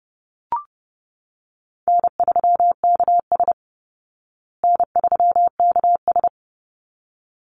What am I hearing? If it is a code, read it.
Morse code, heard as a keyed tone, about 30 wpm, 700 Hz. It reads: N3KS N3KS
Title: Easy Call Signs Repeated 2x - 30wpm